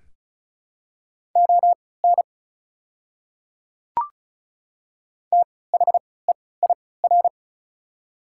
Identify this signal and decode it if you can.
Morse code, heard as a keyed tone, about 36 wpm, spaced as for 20 wpm, 700 Hz. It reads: ON THEIR
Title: 3x Character Spacing - Top 200 Words - 35wpm